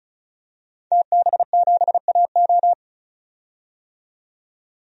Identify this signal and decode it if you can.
Morse code, heard as a keyed tone, about 34 wpm, 700 Hz. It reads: TB7AO
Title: DXCC-100 Call Signs - 5 Characters - 35wpm